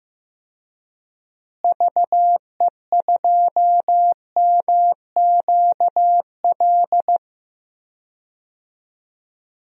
Morse code, heard as a keyed tone, about 15 wpm, 700 Hz. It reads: VE2MQL
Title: DXCC-100 Call Signs - 6 Characters - Repeated 3x - 15wpm